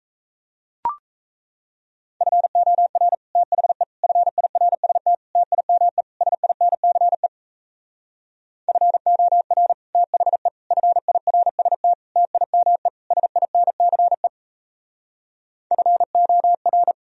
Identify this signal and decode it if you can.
Morse code, heard as a keyed tone, about 38 wpm, 700 Hz. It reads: FOR THE FIRST TIME SINCE FOR THE FIRST TIME SINCE FOR